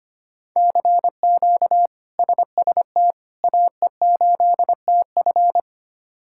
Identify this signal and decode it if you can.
Morse code, heard as a keyed tone, about 25 wpm, 700 Hz. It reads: CQ SST AE8TF
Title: K1USN SST QSO Practice - 25wpm